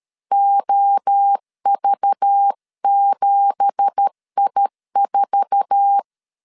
Morse code, heard as a keyed tone, about 13 wpm, 800 Hz. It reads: OV7I4
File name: CW / DET / 1 9 1500C S A 12wpm 3800